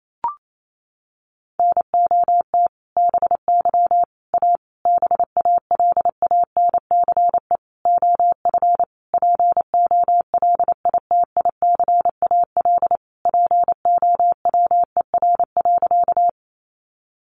Morse code, heard as a keyed tone, about 28 wpm, 700 Hz. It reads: NOT BY A BALANCE OF POLITICAL POWER.
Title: Sentences from Top 900 Words 28wpm